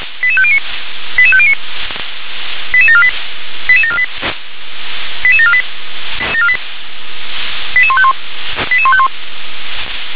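Five-tone selective call (ZVEI), heard as quick runs of five tones, tones of about 70 ms. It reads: 88509 88509 88748 88748 88748 88151 88151
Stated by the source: ZVEI five-tone selective call 88749, received by SDR.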